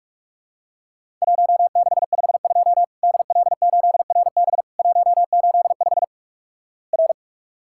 Morse code, heard as a keyed tone, about 47 wpm, 700 Hz. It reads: 1652 DL8RB 185 R
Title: Worked All Europe DX Contest - QTC Exchange - 45wpm